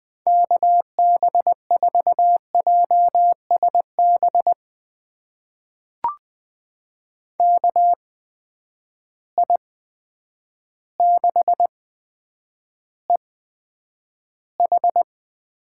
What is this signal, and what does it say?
Morse code, heard as a keyed tone, about 20 wpm, spaced as for 9 wpm, 700 Hz. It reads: KB4JSB KI6EH
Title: USA Call Signs - 6 Characters - 8x Character Spacing - 20wpm